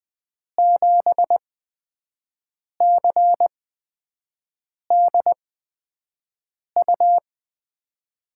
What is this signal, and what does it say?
Morse code, heard as a keyed tone, about 20 wpm, spaced as for 5 wpm, 700 Hz. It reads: 7CDU
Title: USA Call Signs - 6 Characters - 8x Character Spacing - 20wpm